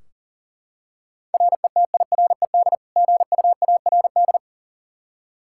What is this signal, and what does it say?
Morse code, heard as a keyed tone, about 40 wpm, 700 Hz. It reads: RETIRED GUARD
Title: Practice QSO - Exchange 3 - More Info - 40wpm